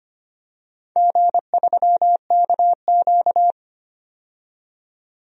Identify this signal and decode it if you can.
Morse code, heard as a keyed tone, about 25 wpm, 700 Hz. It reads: G3KQ
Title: DXCC-100 Call Signs - 4 Characters - 25wpm